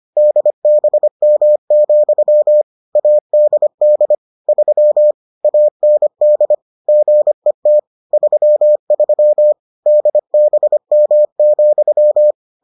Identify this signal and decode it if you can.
Morse code, heard as a keyed tone, about 25 wpm, 600 Hz. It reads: DBM, ADD 3 AND GET 33 DBM,